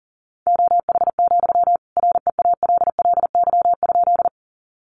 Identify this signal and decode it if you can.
Morse code, heard as a keyed tone, about 40 wpm, 700 Hz. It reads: OH, REALLY?
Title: Sentences from Top 600 Words 40wpm